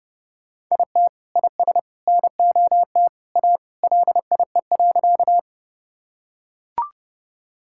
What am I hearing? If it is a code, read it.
Morse code, heard as a keyed tone, about 32 wpm, 700 Hz. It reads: IT IS NOT A LIE.
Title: Sentences from Top 1600 Words 30wpm